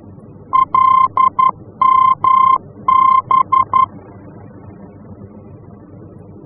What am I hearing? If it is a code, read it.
Morse code, heard as a keyed tone, about 11 wpm, 1050 Hz. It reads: LMB
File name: france / mx / LMB-F-VOR